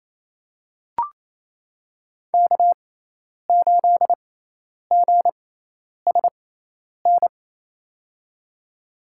Morse code, heard as a keyed tone, about 28 wpm, spaced as for 10 wpm, 700 Hz. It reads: K8GSN
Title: USA Call Signs - 5 Characters - 6x Character Spacing - 28wpm